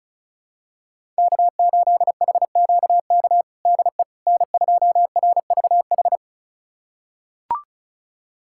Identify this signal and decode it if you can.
Morse code, heard as a keyed tone, about 34 wpm, 700 Hz. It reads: K8HQK DE N2RVH
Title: USA Callsign DE Callsign - 5 Characters - 35wpm